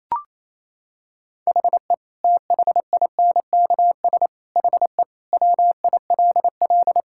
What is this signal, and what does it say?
Morse code, heard as a keyed tone, about 28 wpm, 700 Hz. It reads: HE THINKS HE WILL
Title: Sentences from Top 1000 Words - Review Part 6 - 28wpm